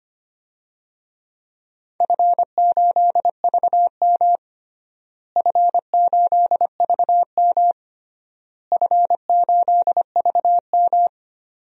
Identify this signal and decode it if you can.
Morse code, heard as a keyed tone, about 25 wpm, 700 Hz. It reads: F8VM F8VM F8VM
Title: DXCC-100 Call Signs - 4 Characters - Repeated 3x - 25wpm